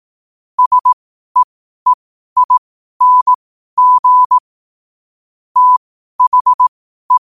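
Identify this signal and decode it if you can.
Morse code, heard as a keyed tone, about 18 wpm, spaced as for 12 wpm, 1000 Hz. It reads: SEEING THE
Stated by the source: Morse code at 12 wpm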